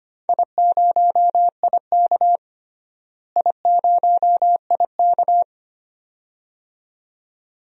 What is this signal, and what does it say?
Morse code, heard as a keyed tone, about 25 wpm, 700 Hz. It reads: I0IK I0IK
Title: DXCC-100 Call Signs - 4 Characters - Repeated 2x - 25wpm